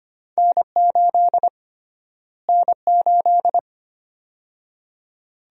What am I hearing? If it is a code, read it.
Morse code, heard as a keyed tone, about 25 wpm, 700 Hz. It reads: N8 N8
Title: USA Call Sign Prefixes - Repeated 2x - 25wpm